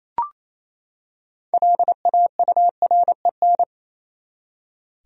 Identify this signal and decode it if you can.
Morse code, heard as a keyed tone, about 28 wpm, 700 Hz. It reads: LAUREN
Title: USA Common Names - 28wpm